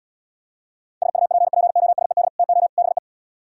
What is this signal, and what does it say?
Morse code, heard as a keyed tone, about 77 wpm, 700 Hz. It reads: SUPPOSED TO BE